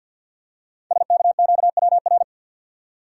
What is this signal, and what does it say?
Morse code, heard as a keyed tone, about 46 wpm, 700 Hz. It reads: SKQWR
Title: Code Groups - 5 Letter:Numbers - 50wpm